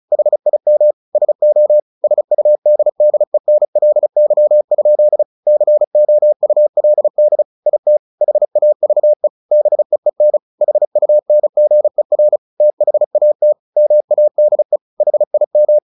Morse code, heard as a keyed tone, about 34 wpm, 600 Hz. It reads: HIM SO SUDDENLY? COULD IT HAVE BEEN HUNGER THAT MADE HIM
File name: the old man and the sea 2 35wpm